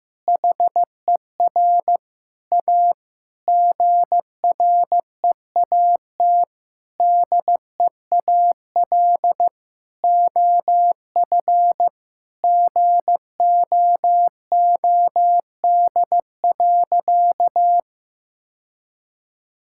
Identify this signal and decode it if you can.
Morse code, heard as a keyed tone, about 16 wpm, 700 Hz. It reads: HER A GREAT DEAL OF GOOD.
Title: Sentences from Top 1000 Words - Review Part 6 - 15wpm